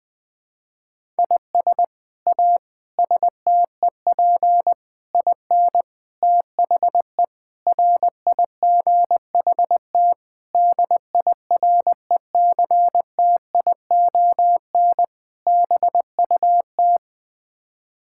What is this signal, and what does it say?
Morse code, heard as a keyed tone, about 20 wpm, 700 Hz. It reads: IS A STEP IN THE RIGHT DIRECTION BUT
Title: Sets of 8 Words - 20wpm